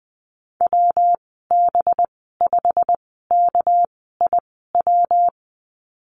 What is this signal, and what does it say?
Morse code, heard as a keyed tone, about 20 wpm, spaced as for 15 wpm, 700 Hz. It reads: WB5KIW
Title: 200 Lesson 32g - Letter k - Call Signs 20wpm